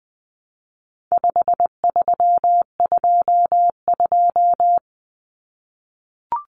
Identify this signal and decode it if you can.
Morse code, heard as a keyed tone, about 20 wpm, 700 Hz. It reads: 5322
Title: Common Cut Numbers - 4-Digit 20wpm